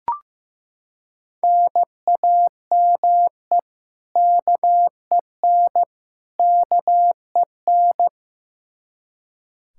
Morse code, heard as a keyed tone, about 15 wpm, 700 Hz. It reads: NAME KEN KEN
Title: Practice QSO - Exchange 1 - Beginning - 15wpm